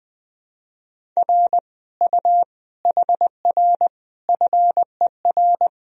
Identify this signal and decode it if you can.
Morse code, heard as a keyed tone, about 20 wpm, 700 Hz. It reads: R U HR FER
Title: Sentences from Top 200 Words and Abbreviations 20wpm